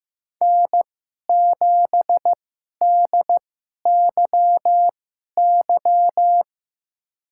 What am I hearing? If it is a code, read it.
Morse code, heard as a keyed tone, about 15 wpm, spaced as for 11 wpm, 700 Hz. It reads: N7DYY
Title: USA Call Signs - 5 Characters - 2x Character Spacing - 15wpm